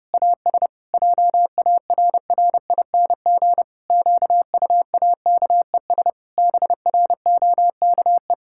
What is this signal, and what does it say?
Morse code, heard as a keyed tone, about 31 wpm, 700 Hz. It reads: AS JARRING QUAKES BROKE
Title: Pangram Challenge v2 - 30wpm